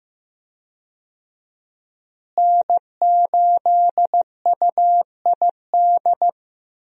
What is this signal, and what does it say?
Morse code, heard as a keyed tone, about 15 wpm, 700 Hz. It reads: N8UID
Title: USA Call Signs - 5 Characters - Repeated 2x - 15wpm